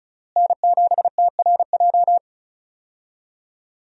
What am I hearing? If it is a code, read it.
Morse code, heard as a keyed tone, about 36 wpm, 700 Hz. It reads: N7TRJ
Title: USA Call Signs - 5 Characters - Repeated 3x - 35wpm